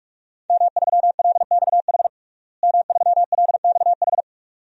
Morse code, heard as a keyed tone, about 43 wpm, 700 Hz. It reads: M3LXH M3LXH
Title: DXCC-100 Call Signs - 5 Characters - Repeated 2x - 45wpm